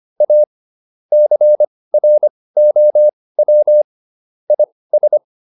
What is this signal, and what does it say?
Morse code, heard as a keyed tone, about 25 wpm, spaced as for 18 wpm, 600 Hz. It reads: A CROW IS